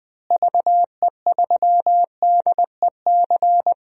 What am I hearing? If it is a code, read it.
Morse code, heard as a keyed tone, about 20 wpm, 700 Hz. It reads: VE3DEC
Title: DXCC-100 Call Signs - 6 Characters - Repeated 3x - 20wpm